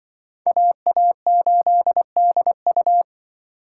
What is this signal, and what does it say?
Morse code, heard as a keyed tone, about 24 wpm, 700 Hz. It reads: AA8DU
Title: Speed Racing - USA 2x2 Call Signs 20wpm